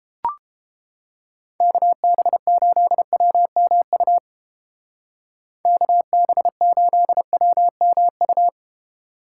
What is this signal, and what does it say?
Morse code, heard as a keyed tone, about 29 wpm, 700 Hz. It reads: KB8WMU KB8WMU